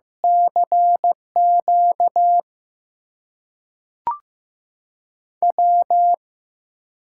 Morse code, heard as a keyed tone, about 15 wpm, spaced as for 5 wpm, 700 Hz. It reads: CQW
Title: USA Call Signs - 6 Characters - 8x Character Spacing - 15wpm